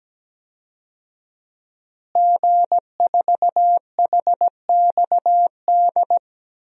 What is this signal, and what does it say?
Morse code, heard as a keyed tone, about 17 wpm, 700 Hz. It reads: G4HXD
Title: DXCC-100 Call Signs - 5 Characters - Repeated 2x - 17wpm